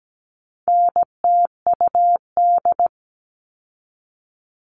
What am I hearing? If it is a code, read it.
Morse code, heard as a keyed tone, about 17 wpm, 700 Hz. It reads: NTUD
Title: All Cut Numbers - 4-Digit 17wpm